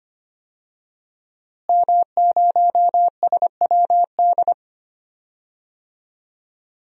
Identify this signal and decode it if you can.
Morse code, heard as a keyed tone, about 25 wpm, 700 Hz. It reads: M0SWD
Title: DXCC-100 Call Signs - 5 Characters - Repeated 2x - 25wpm